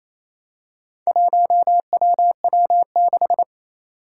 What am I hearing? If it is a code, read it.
Morse code, heard as a keyed tone, about 29 wpm, 700 Hz. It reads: 1WW6
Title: Four Challenging Letter-Number Combinations 28wpm